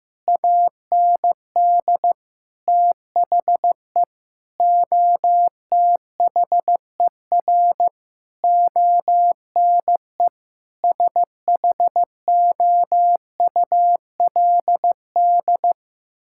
Morse code, heard as a keyed tone, about 15 wpm, 700 Hz. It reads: AND THE OTHER ONE SHOULD
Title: Sentences from Top 1600 Words 15wpm